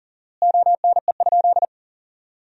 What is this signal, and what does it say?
Morse code, heard as a keyed tone, about 40 wpm, 700 Hz. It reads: ONE?
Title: 4x Word-Spacing - Sentences from Top 300 Words - 40wpm